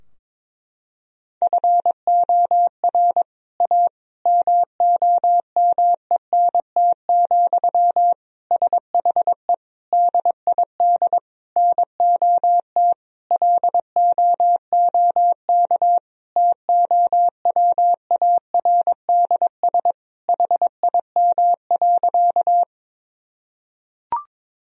Morse code, heard as a keyed tone, about 22 wpm, 700 Hz. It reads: FOR A MOMENT, SHE DID NOT LOOK TOWARDS HIM.
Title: Sentences from Top 1000 Words - Review Part 2 - 22wpm